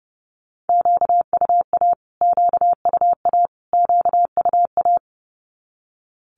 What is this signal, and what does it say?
Morse code, heard as a keyed tone, about 30 wpm, 700 Hz. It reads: QUA QUA QUA
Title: Q-Codes Repeated 3x 30wpm